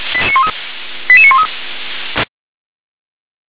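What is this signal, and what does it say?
Five-tone selective call (ZVEI), heard as quick runs of five tones, tones of about 70 ms. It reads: 88913
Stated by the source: ZVEI five-tone selective call 88851 received on an SDR.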